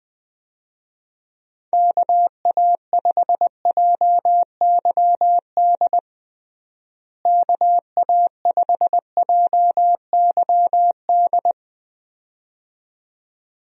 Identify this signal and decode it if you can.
Morse code, heard as a keyed tone, about 21 wpm, 700 Hz. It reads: KA5JYD KA5JYD
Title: USA Call Signs - 6 Characters - Repeated 2x - 20wpm